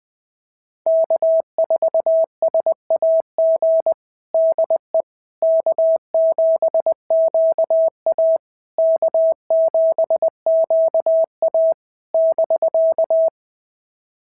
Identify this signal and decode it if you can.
Morse code, heard as a keyed tone, about 20 wpm, 650 Hz. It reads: K4SAG DE K7QA K7QA <BK>
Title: SOTA QSO Practice 20wpm